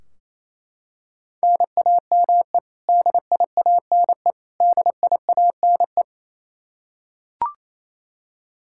Morse code, heard as a keyed tone, about 28 wpm, 700 Hz. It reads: NAME DIANE DIANE
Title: Practice QSO - Exchange 1 - Beginning - 28wpm